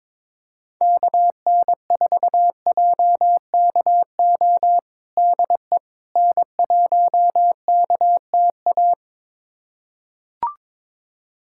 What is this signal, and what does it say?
Morse code, heard as a keyed tone, about 22 wpm, 700 Hz. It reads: KN4JKO DE N1KTA
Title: USA Callsign DE Callsign 22wpm